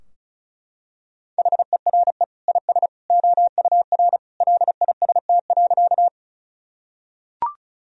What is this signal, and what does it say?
Morse code, heard as a keyed tone, about 36 wpm, 700 Hz. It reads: HERE IS OUR LIST.
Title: Sentences from Top 1000 Words - Review Part 2 - 35wpm